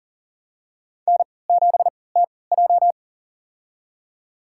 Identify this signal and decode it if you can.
Morse code, heard as a keyed tone, about 40 wpm, spaced as for 23 wpm, 700 Hz. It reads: N7TJ
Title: USA Call Signs - 3x Character Spacing - 40wpm